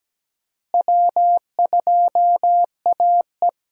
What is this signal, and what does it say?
Morse code, heard as a keyed tone, about 17 wpm, 700 Hz. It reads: W2AE
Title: USA Call Signs - 4 Characters - Repeated 2x - 17wpm